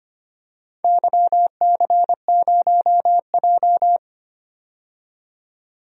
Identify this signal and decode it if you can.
Morse code, heard as a keyed tone, about 25 wpm, 700 Hz. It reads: YC0J
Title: DXCC-100 Call Signs - 4 Characters - 25wpm